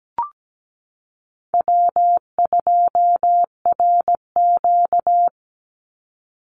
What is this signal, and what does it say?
Morse code, heard as a keyed tone, about 17 wpm, 700 Hz. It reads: W2RQ
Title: Call Signs Repeated 2x - 17wpm